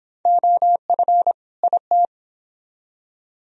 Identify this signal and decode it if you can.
Morse code, heard as a keyed tone, about 29 wpm, 700 Hz. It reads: OF IT